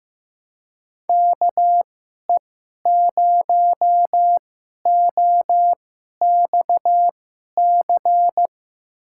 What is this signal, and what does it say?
Morse code, heard as a keyed tone, about 15 wpm, spaced as for 11 wpm, 700 Hz. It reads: KE0OXC
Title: USA Call Signs - 6 Characters - 2x Character Spacing - 15wpm